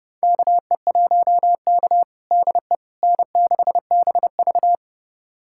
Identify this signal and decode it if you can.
Morse code, heard as a keyed tone, about 30 wpm, 700 Hz. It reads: KE1K DE N6BV
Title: USA Callsign DE Callsign - 4 Characters - 30wpm